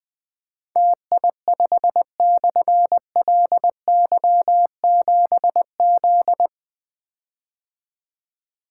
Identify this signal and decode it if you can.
Morse code, heard as a keyed tone, about 21 wpm, 700 Hz. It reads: TI5/LY7Z
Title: Difficult Call Signs 20wpm